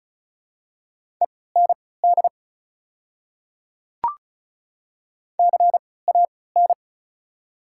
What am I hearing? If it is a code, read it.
Morse code, heard as a keyed tone, about 36 wpm, spaced as for 20 wpm, 700 Hz. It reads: END CAN
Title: 3x Character Spacing - Top 200 Words - 35wpm